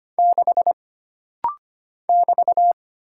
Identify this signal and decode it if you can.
Morse code, heard as a keyed tone, about 25 wpm, 700 Hz. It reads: 6 <BT>
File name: Single Character (Letter:Number:Punctuation:Prosign) - Rapid-Fire - 25wpm